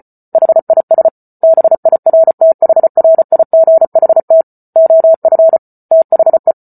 Morse code, heard as a keyed tone, about 34 wpm, 650 Hz. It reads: HIS BIRTHRIGHT OF THE
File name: Book Two / 08-Chapter 8 - 34 WPM